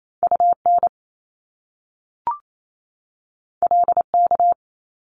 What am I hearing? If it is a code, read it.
Morse code, heard as a keyed tone, about 28 wpm, 700 Hz. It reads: UN LK
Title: Top Two Letter Combinations in Words 28wpm